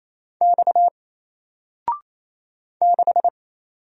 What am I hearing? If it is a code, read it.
Morse code, heard as a keyed tone, about 27 wpm, 700 Hz. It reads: X 6
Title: Single Character (Letter:Number:Punctuation:Prosign) 28wpm